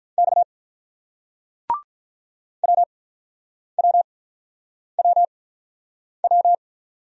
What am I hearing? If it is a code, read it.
Morse code, heard as a keyed tone, about 43 wpm, spaced as for 9 wpm, 700 Hz. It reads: X WWWW
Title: Speed-Racing Single Letters 35wpm